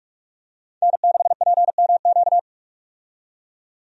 Morse code, heard as a keyed tone, about 44 wpm, 700 Hz. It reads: N6PMQ
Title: USA Call Signs - 5 Characters - 45wpm